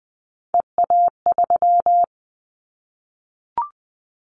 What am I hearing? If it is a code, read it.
Morse code, heard as a keyed tone, about 20 wpm, 700 Hz. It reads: EA3
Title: Callsign Prefixes for Top 25 DXCC - 20wpm